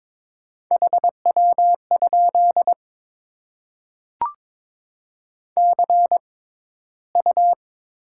Morse code, heard as a keyed tone, about 21 wpm, spaced as for 13 wpm, 690 Hz. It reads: HW? CU
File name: Top 20 QSO Elements - 6x Character Spacing - 22wpm